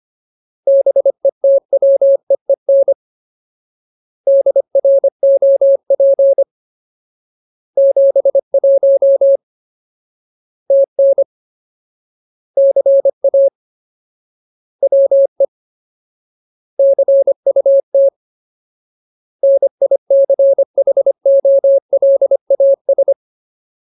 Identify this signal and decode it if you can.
Morse code, heard as a keyed tone, about 25 wpm, 550 Hz. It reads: BETWEEN DROP 71 TN CA WE CUT NICHOLAS
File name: Ultimate Word List / 25WPM Ultimate